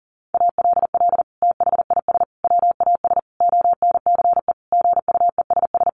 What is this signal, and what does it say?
Morse code, heard as a keyed tone, about 40 wpm, 700 Hz. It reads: ALL THIS WAS ONCE GUESS
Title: Sentences from Top 1500 Words 40wpm